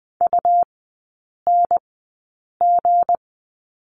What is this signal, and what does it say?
Morse code, heard as a keyed tone, about 20 wpm, spaced as for 8 wpm, 700 Hz. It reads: UNG
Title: 178 Lesson 29e - Slash - Three Characters 20wpm